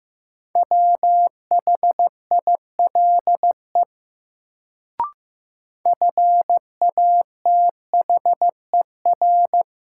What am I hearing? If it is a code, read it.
Morse code, heard as a keyed tone, about 15 wpm, 700 Hz. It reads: WHILE FATHER
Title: Sentences from Top 300 Words - New Word Drill - 15wpm